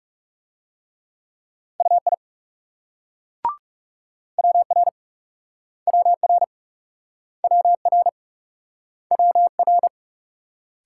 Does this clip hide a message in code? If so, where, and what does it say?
Yes: Morse code from 1.8 to 9.9 s, heard as a keyed tone, about 38 wpm, 700 Hz. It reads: UI WR WR WR WR